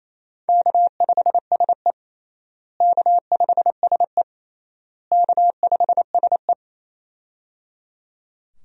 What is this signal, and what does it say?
Morse code, heard as a keyed tone, about 28 wpm, 700 Hz. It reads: K5SE K5SE K5SE
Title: USA Call Signs - 4 Characters - Repeated 3x - 28wpm